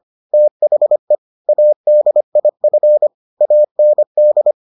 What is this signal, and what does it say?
Morse code, heard as a keyed tone, about 25 wpm, 600 Hz. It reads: THE ADIF AND